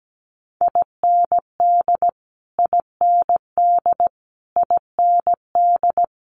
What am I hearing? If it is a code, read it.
Morse code, heard as a keyed tone, about 17 wpm, 700 Hz. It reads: IND IND IND
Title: Multi-letter Phonemes Repeated 3x 17wpm